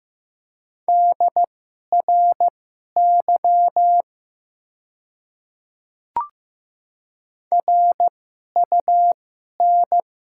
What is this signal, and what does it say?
Morse code, heard as a keyed tone, about 15 wpm, spaced as for 11 wpm, 700 Hz. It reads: DRY RUN
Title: 2x Character Spacing - Top 500 Words - 15wpm